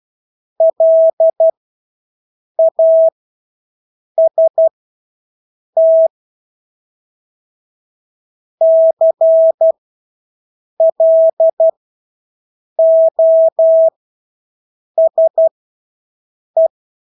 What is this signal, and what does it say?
Morse code, heard as a keyed tone, about 12 wpm, spaced as for 6 wpm, 650 Hz. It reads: LAST CLOSE
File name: Engelske ord 12-6 wpm